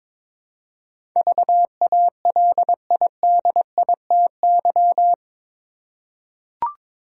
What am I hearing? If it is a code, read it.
Morse code, heard as a keyed tone, about 22 wpm, 700 Hz. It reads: VALIDITY
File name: practice / 22wpm / Eight Letter Words - 22wpm